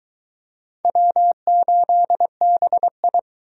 Morse code, heard as a keyed tone, about 22 wpm, 700 Hz. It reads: W8BI